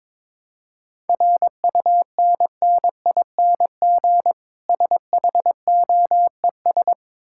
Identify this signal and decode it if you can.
Morse code, heard as a keyed tone, about 22 wpm, 690 Hz. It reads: RUNNING SHOES
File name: Difficult Words with Hints - 22wpm